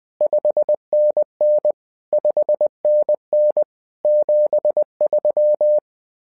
Morse code, heard as a keyed tone, about 20 wpm, 600 Hz. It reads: 5NN 5NN 73
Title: SOTA QSO Practice 20wpm